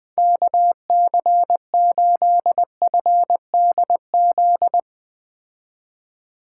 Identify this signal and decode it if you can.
Morse code, heard as a keyed tone, about 20 wpm, 700 Hz. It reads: KC8FDZ
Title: Speed Racing - USA 2x3 Call Signs 17wpm